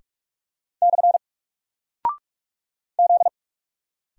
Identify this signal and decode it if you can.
Morse code, heard as a keyed tone, about 44 wpm, 700 Hz. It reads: / Z
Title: Single Character (Letter:Number:Punctuation:Prosign) 45wpm